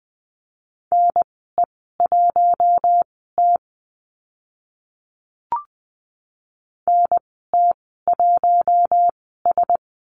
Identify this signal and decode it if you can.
Morse code, heard as a keyed tone, about 20 wpm, spaced as for 14 wpm, 700 Hz. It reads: NE1T NT1S
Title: 042 Lesson 8g - Number 1 - Call Signs 20wpm